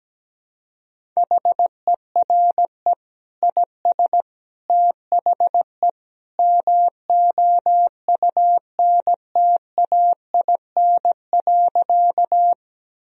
Morse code, heard as a keyed tone, about 17 wpm, 700 Hz. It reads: HERE IS THE MOUNTAIN.
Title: Sentences from Top 1000 Words - Review Part 1 - 17wpm